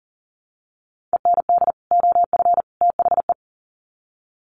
Morse code, heard as a keyed tone, about 38 wpm, 700 Hz. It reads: END OF THE
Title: Sets of 3 Words 40wpm